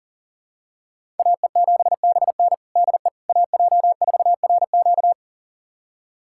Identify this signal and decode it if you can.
Morse code, heard as a keyed tone, about 40 wpm, 700 Hz. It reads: AE7BN DE AJ4RQ
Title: USA Callsign DE Callsign - 5 Characters - 40wpm